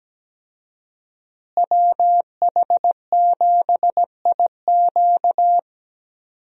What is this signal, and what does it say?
Morse code, heard as a keyed tone, about 17 wpm, 700 Hz. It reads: WH7IQ
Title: USA Call Signs - 5 Characters - Repeated 3x - 17wpm